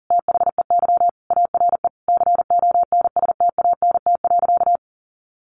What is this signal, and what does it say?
Morse code, heard as a keyed tone, about 40 wpm, 700 Hz. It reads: THEY ARE CONSTANT.
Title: Sentences from Top 1500 Words 40wpm